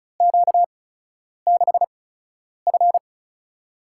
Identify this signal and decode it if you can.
Morse code, heard as a keyed tone, about 36 wpm, spaced as for 10 wpm, 700 Hz. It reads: Q6F
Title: USA Call Signs - 8x Character Spacing - 35wpm